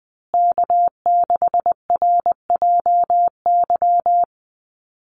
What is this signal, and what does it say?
Morse code, heard as a keyed tone, about 20 wpm, 700 Hz. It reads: K6RJY
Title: USA Call Signs Repeated 2x 20wpm